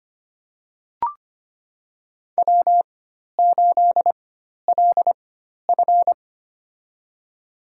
Morse code, heard as a keyed tone, about 25 wpm, spaced as for 12 wpm, 700 Hz. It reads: W8LF
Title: USA Call Signs - 4 Characters - 4x Character Spacing - 25wpm